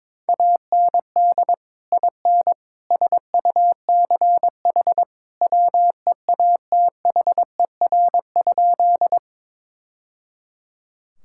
Morse code, heard as a keyed tone, about 22 wpm, 700 Hz. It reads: AND IN SUCH WEATHER?
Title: Sentences from Top 1000 Words - Review Part 5 - 22wpm